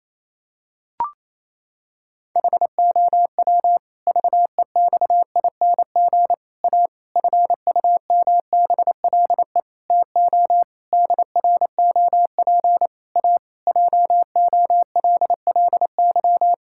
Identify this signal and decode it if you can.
Morse code, heard as a keyed tone, about 28 wpm, 700 Hz. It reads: HOW VEXING A FUMBLE TO DROP A JOLLY
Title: Pangram Challenge v2 - 28wpm